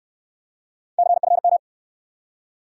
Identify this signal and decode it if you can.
Morse code, heard as a keyed tone, about 68 wpm, 700 Hz. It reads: CFM